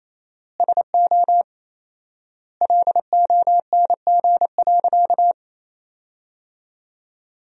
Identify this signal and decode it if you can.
Morse code, heard as a keyed tone, about 27 wpm, 700 Hz. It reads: SO LONG.
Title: 4x Word-Spacing - Sentences from Top 100 Words - 28wpm